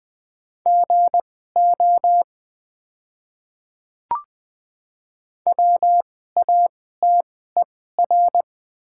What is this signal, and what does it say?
Morse code, heard as a keyed tone, about 20 wpm, spaced as for 15 wpm, 700 Hz. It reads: GO WATER
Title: 2x Character Spacing - Top 100 Words - 20wpm